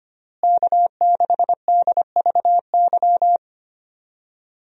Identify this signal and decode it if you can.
Morse code, heard as a keyed tone, about 25 wpm, 700 Hz. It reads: K6DVY
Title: USA Call Signs - 5 Characters - Repeated 3x - 25wpm